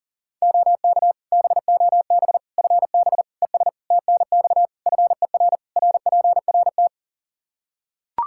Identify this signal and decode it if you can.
Morse code, heard as a keyed tone, about 40 wpm, 700 Hz. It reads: OK BOB FB ES TNX FER RPRT
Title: Practice QSO - Exchange 2 - Info - 40wpm